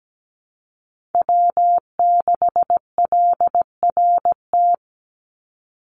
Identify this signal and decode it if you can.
Morse code, heard as a keyed tone, about 17 wpm, 700 Hz. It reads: W6LRT
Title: USA Call Signs 17wpm